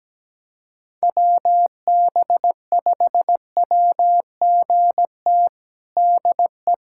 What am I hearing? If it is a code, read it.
Morse code, heard as a keyed tone, about 17 wpm, 700 Hz. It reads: WB5WGT DE
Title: USA Callsign DE Callsign 17wpm